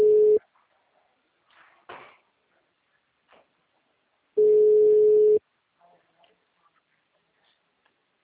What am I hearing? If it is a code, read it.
Ringback tone (Europe, 425 Hz).
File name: out- -2003- - - .163